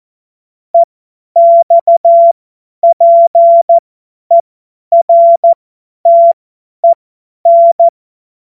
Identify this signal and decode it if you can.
Morse code, heard as a keyed tone, about 14 wpm, spaced as for 10 wpm, 680 Hz. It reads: EXPERTEN
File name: Morsekurs / Witze / Witz 12 Dezember 2023 50 F70